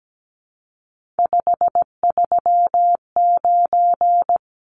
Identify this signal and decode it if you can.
Morse code, heard as a keyed tone, about 17 wpm, 700 Hz. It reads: 539
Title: Numbers - 3-Digit 17wpm